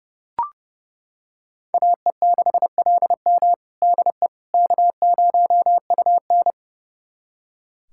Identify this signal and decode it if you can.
Morse code, heard as a keyed tone, about 30 wpm, 700 Hz. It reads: AE6LM DE K0UN
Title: USA Callsign DE Callsign 30wpm